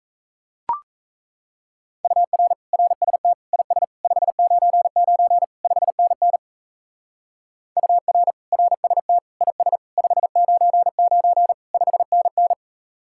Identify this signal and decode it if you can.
Morse code, heard as a keyed tone, about 36 wpm, 700 Hz. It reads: UR RST IS 599 5NN UR RST IS 599 5NN